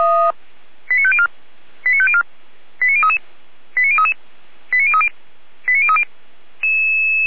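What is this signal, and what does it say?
Five-tone selective call (ZVEI), heard as quick runs of five tones, tones of about 70 ms. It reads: 89504 89504 89033 89033 89030 89030
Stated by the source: ZVEI five-tone call received by SDR, from the feuerwehr-orlamuende.org fire brigade audio archive.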